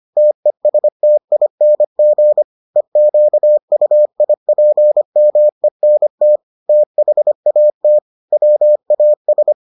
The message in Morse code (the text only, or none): TESTING EQUIPMENT THAT WAS